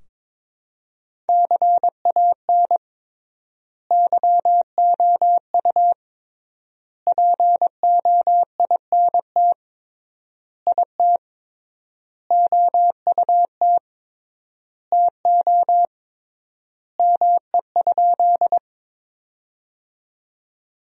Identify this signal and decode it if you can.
Morse code, heard as a keyed tone, about 22 wpm, 700 Hz. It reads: CAN YOU POINT IT OUT TO ME?
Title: 3x Word-Spacing - Sentences from Top 200 Words - 22wpm